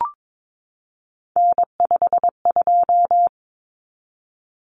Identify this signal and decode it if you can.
Morse code, heard as a keyed tone, about 22 wpm, 700 Hz. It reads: N52
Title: Common Cut Numbers - 3-Digit 22wpm